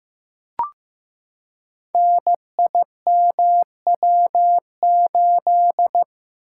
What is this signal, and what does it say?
Morse code, heard as a keyed tone, about 15 wpm, 700 Hz. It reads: NIMW8
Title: Code Groups - 5 Letter:Numbers - 15wpm